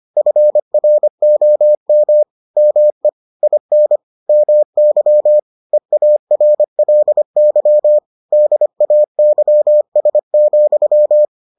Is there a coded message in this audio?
Yes: Morse code from 0.2 to 11.2 s, heard as a keyed tone, about 25 wpm, 600 Hz. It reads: FROM ME IN MY EARLY DAYS,